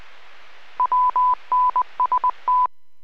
Morse code, heard as a keyed tone, about 20 wpm, 1000 Hz. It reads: WNST